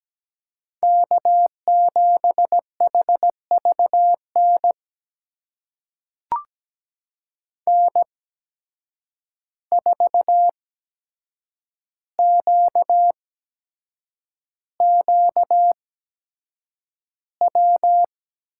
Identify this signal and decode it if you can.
Morse code, heard as a keyed tone, about 17 wpm, spaced as for 7 wpm, 700 Hz. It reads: K7HVN N4QQW
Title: USA Call Signs - 5 Characters - 8x Character Spacing - 17wpm